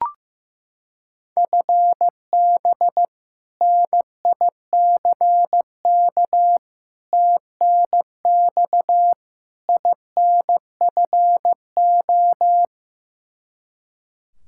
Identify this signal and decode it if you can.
Morse code, heard as a keyed tone, about 16 wpm, 700 Hz. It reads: FB NICK TNX INFO
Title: Practice QSO - Exchange 3 - More Info - 15wpm